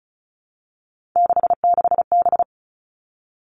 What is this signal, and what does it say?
Morse code, heard as a keyed tone, about 34 wpm, 700 Hz. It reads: 66B
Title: All Cut Numbers - 3-Digit 35wpm